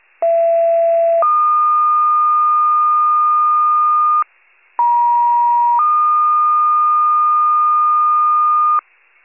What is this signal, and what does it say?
Two-tone page, heard as a one-second tone, then a three-second tone, about 670 then 1220 Hz, about 950 then 1220 Hz, with noise. It is heard from 0.2 to 8.8 s.